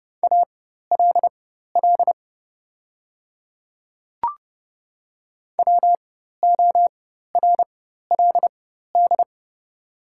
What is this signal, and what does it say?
Morse code, heard as a keyed tone, about 32 wpm, spaced as for 14 wpm, 700 Hz. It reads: ALL WORLD
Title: 4x Character Spacing - Top 200 Words - 30wpm